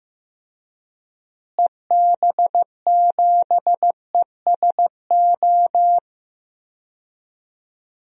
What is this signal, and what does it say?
Morse code, heard as a keyed tone, about 15 wpm, 700 Hz. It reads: EB7ESO
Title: DXCC-100 Call Signs - 6 Characters - Repeated 2x - 15wpm